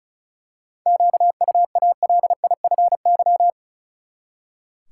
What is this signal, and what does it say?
Morse code, heard as a keyed tone, about 34 wpm, 700 Hz. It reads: QUALIFY
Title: Seven Letter Words - 35wpm